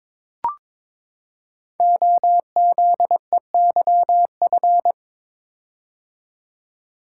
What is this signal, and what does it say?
Morse code, heard as a keyed tone, about 22 wpm, 700 Hz. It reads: OZEYF
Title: Code Groups - 5 Letter:Numbers - 22wpm